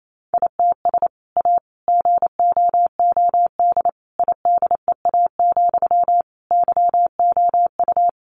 Morse code, heard as a keyed tone, about 28 wpm, 700 Hz. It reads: ITS A GOOD IDEA, YOU
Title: Sentences from Top 600 Words 28wpm